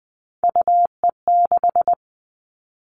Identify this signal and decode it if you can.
Morse code, heard as a keyed tone, about 20 wpm, 700 Hz. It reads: UE6
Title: All Cut Numbers - 3-Digit 20wpm